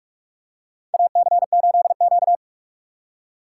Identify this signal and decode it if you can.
Morse code, heard as a keyed tone, about 46 wpm, 700 Hz. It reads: AC8Q
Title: USA Call Signs - 6x Character Spacing - 45wpm